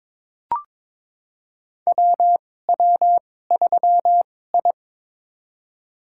Morse code, heard as a keyed tone, about 22 wpm, spaced as for 16 wpm, 700 Hz. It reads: WW3I
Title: USA Call Signs - 2x Character Spacing - 22wpm